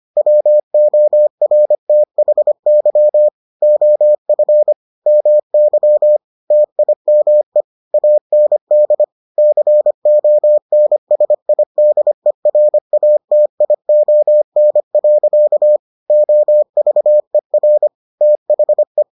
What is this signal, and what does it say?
Morse code, heard as a keyed tone, about 25 wpm, 600 Hz. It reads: WORTHY OF MY TIME AND CONSIDERATION. OVER THE